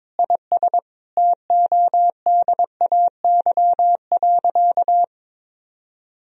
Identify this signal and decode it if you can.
Morse code, heard as a keyed tone, about 22 wpm, 700 Hz. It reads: IS TODAY.
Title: Sentences from Top 1000 Words - Review Part 3 - 22wpm